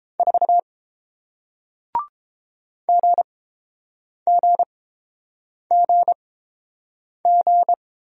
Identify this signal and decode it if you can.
Morse code, heard as a keyed tone, about 29 wpm, spaced as for 7 wpm, 700 Hz. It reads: 4 GGGG